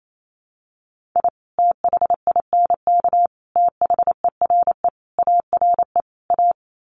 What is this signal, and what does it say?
Morse code, heard as a keyed tone, about 28 wpm, 700 Hz. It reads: I THINK THERE ARE A
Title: Sets of 5 Words Encore2 28wpm